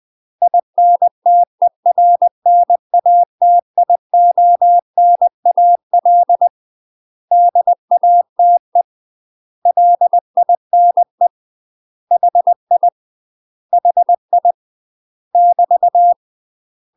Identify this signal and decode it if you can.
Morse code, heard as a keyed tone, about 20 wpm, 700 Hz. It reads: INTERNATIONAL DATE LINE HI HI <BT>
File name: UK MASTER QSOs TRAINING/20WPM